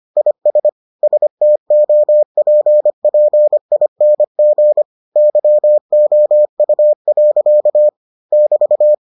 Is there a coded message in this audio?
Yes: Morse code from 0.2 to 8.9 s, heard as a keyed tone, about 25 wpm, 600 Hz. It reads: IS STOPPING YOU. <BT>